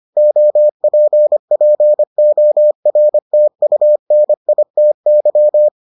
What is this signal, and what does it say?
Morse code, heard as a keyed tone, about 25 wpm, 600 Hz. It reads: OPPORTUNITY